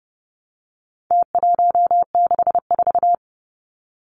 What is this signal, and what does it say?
Morse code, heard as a keyed tone, about 30 wpm, 700 Hz. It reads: T164
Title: Numbers - Contest Style - 4-Digit 30wpm